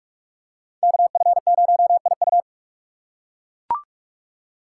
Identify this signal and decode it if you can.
Morse code, heard as a keyed tone, about 41 wpm, 700 Hz. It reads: KF0IU
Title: USA Call Signs - 4x Character Spacing - 45wpm